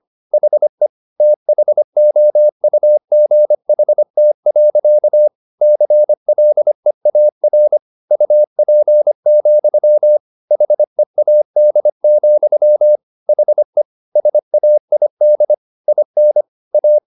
Morse code, heard as a keyed tone, about 25 wpm, 600 Hz. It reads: HE THOUGHT. CLEAR UP, HEAD, HE SAID IN A